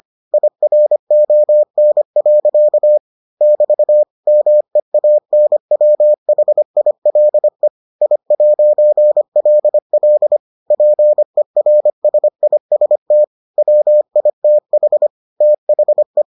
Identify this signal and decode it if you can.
Morse code, heard as a keyed tone, about 25 wpm, 600 Hz. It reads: IRON. <BT> MEANWHILE I'LL PERSIST WITH THE